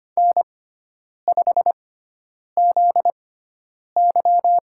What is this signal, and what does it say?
Morse code, heard as a keyed tone, about 25 wpm, spaced as for 9 wpm, 700 Hz. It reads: N5ZY
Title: USA Call Signs - 4 Characters - 6x Character Spacing - 25wpm